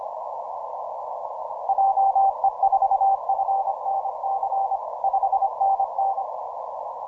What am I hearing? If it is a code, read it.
Morse code, heard as a keyed tone, about 25 wpm, 780 Hz. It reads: JE4FNC 5NN
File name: simple-file-list / VK9NT 12MCW